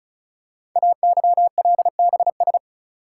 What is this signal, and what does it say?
Morse code, heard as a keyed tone, about 36 wpm, 700 Hz. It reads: AYLBS
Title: Code Groups - 5 Letter:Numbers - 35wpm